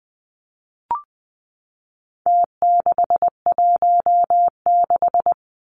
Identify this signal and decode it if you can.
Morse code, heard as a keyed tone, about 20 wpm, 700 Hz. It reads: T616
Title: Numbers - Contest Style - 4-Digit 20wpm